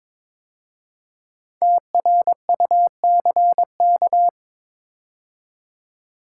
Morse code, heard as a keyed tone, about 23 wpm, 690 Hz. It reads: TRUCK
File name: 5 Letter Words - First 3 Letters Given - 22wpm